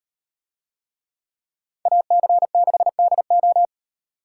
Morse code, heard as a keyed tone, about 38 wpm, 700 Hz. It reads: AC6DO